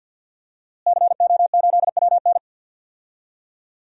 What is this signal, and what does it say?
Morse code, heard as a keyed tone, about 46 wpm, 700 Hz. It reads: CO8WN
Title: DXCC-100 Call Signs - 5 Characters - Repeated 2x - 50wpm